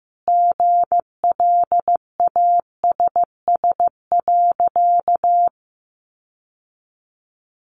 Morse code, heard as a keyed tone, about 15 wpm, 700 Hz. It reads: GLASS.
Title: Sentences from Top 1300 Words 15wpm